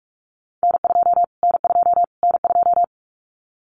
Morse code, heard as a keyed tone, about 44 wpm, 700 Hz. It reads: N2 N2 N2
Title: Callsign Prefixes for Top 25 DXCC - Repeated 3x - 45wpm